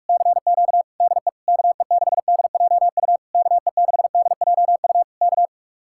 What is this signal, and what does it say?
Morse code, heard as a keyed tone, about 59 wpm, spaced as for 52 wpm, 700 Hz. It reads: CQ DE KE6DJU KE6DJU K
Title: Speed-Racing Common QSO Phrases - 45wpm